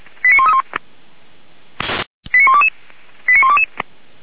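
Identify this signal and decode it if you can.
Five-tone selective call (ZVEI), heard as quick runs of five tones, tones of about 70 ms. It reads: 89131 89122 89122
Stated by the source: ZVEI five-tone call 89132 received by SDR, from feuerwehr-orlamuende (fire brigade Orlamünde) server.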